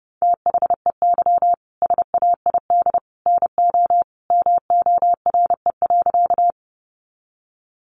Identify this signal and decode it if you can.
Morse code, heard as a keyed tone, about 30 wpm, 700 Hz. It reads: THEY SAID NO MORE.
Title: Sentences from Top 100 Words 30wpm